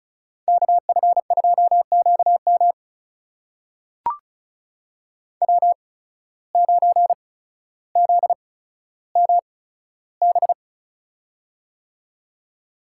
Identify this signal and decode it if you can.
Morse code, heard as a keyed tone, about 36 wpm, spaced as for 15 wpm, 700 Hz. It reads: KF2QM W9ZMB
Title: USA Call Signs - 5 Characters - 8x Character Spacing - 35wpm